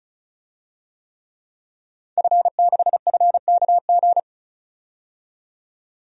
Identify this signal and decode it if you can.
Morse code, heard as a keyed tone, about 36 wpm, 700 Hz. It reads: F6FKG
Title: DXCC-100 Call Signs - 5 Characters - 35wpm